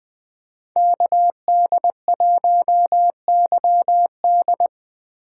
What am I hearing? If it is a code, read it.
Morse code, heard as a keyed tone, about 20 wpm, 700 Hz. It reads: KD1YD
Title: USA Call Signs - 5 Characters - 2x Character Spacing - 20wpm